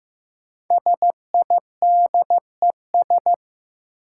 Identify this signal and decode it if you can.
Morse code, heard as a keyed tone, about 15 wpm, 700 Hz. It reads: SIDES
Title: Sets of 4 Words - New Word Drill - 15wpm